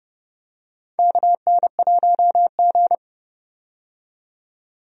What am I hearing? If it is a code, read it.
Morse code, heard as a keyed tone, about 32 wpm, 700 Hz. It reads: KN1G
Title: USA Call Signs - 4 Characters - 30wpm